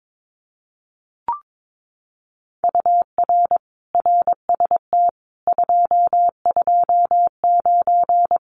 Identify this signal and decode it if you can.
Morse code, heard as a keyed tone, about 22 wpm, 700 Hz. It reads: UR RST 229
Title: Common QSO Phrases Encore 22wpm